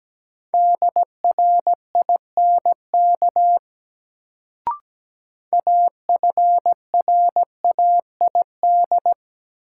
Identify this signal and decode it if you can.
Morse code, heard as a keyed tone, about 17 wpm, 700 Hz. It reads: DRINK AFRAID
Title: Sentences from Top 1500 Words - New Word Drill - 17wpm